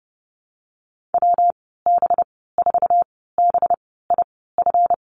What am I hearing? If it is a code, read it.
Morse code, heard as a keyed tone, about 30 wpm, spaced as for 17 wpm, 700 Hz. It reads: WB4BIF
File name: Lesson 37g - Letter x - Call Signs 30wpm